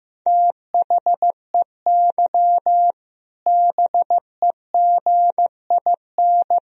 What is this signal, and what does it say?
Morse code, heard as a keyed tone, about 15 wpm, 700 Hz. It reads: THEY BEGIN
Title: Sentences from Top 1000 Words - Review Part 3 - 15wpm